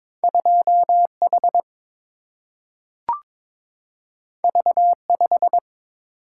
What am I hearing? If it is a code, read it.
Morse code, heard as a keyed tone, about 22 wpm, 700 Hz. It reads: 2 H V 5
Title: Single-Letter Single-Number 22wpm